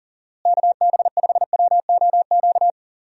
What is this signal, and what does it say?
Morse code, heard as a keyed tone, about 38 wpm, 700 Hz. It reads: KB5WOQ
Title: USA Call Signs - 6 Characters - Repeated 3x - 40wpm